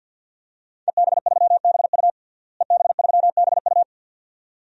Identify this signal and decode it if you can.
Morse code, heard as a keyed tone, about 48 wpm, 700 Hz. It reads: EB3BU EB3BU
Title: DXCC-100 Call Signs - 5 Characters - Repeated 2x - 50wpm